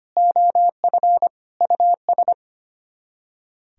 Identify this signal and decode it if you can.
Morse code, heard as a keyed tone, about 32 wpm, 700 Hz. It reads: OF US